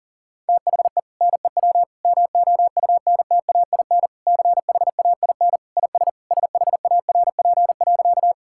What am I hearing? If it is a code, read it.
Morse code, heard as a keyed tone, about 40 wpm, 700 Hz. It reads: THE NEW MOUNTAIN CHAIN IS SHARP.
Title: Sentences from Top 1600 Words 40wpm